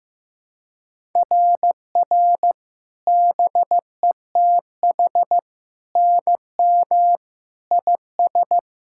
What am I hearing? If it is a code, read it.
Morse code, heard as a keyed tone, about 14 wpm, 700 Hz. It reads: RR BETH NM IS
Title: SOTA QSO Practice 15wpm